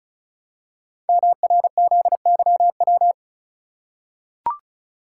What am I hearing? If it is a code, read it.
Morse code, heard as a keyed tone, about 36 wpm, 700 Hz. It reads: MRZYW
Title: Code Groups - 5 Letters - 35wpm